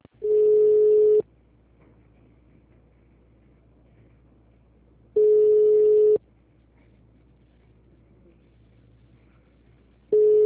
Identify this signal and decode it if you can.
Ringback tone (Europe, 425 Hz).